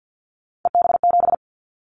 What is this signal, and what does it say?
Morse code, heard as a keyed tone, about 50 wpm, 700 Hz. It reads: EB7
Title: Callsign Prefixes for Top 25 DXCC - 50wpm